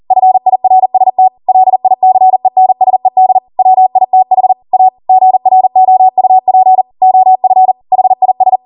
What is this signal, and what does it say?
Morse code, heard as a keyed tone, about 40 wpm, 750 Hz. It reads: FIRST LICENSED WITH A GROUP OF HIS